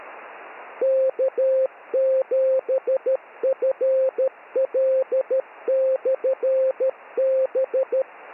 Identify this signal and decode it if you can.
Morse code, heard as a keyed tone, about 13 wpm, 530 Hz, with radio noise. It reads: K7FL/B
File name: n6tr / K7FL 222 Beacon